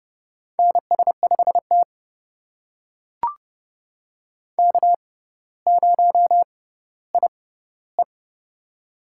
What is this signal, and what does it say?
Morse code, heard as a keyed tone, about 29 wpm, spaced as for 15 wpm, 700 Hz. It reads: NS5T K0IE
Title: USA Call Signs - 4 Characters - 6x Character Spacing - 30wpm